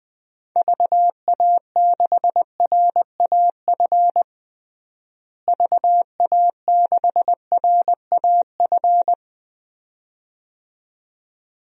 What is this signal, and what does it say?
Morse code, heard as a keyed tone, about 20 wpm, 700 Hz. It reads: VA6RAF VA6RAF
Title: DXCC-100 Call Signs - 6 Characters - Repeated 2x - 20wpm